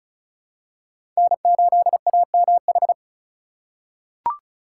Morse code, heard as a keyed tone, about 36 wpm, 700 Hz. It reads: N8AMH
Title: USA Call Signs - 3x Character Spacing - 35wpm